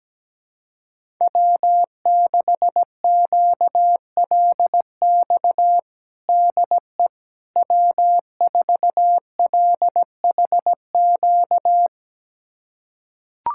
Morse code, heard as a keyed tone, about 17 wpm, 700 Hz. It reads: W6QLX DE W4LHQ
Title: USA Callsign DE Callsign - 5 Characters - 17wpm